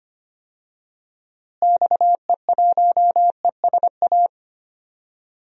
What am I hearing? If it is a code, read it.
Morse code, heard as a keyed tone, about 25 wpm, 700 Hz. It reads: XE1ESA
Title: DXCC-100 Call Signs - 6 Characters - Repeated 2x - 25wpm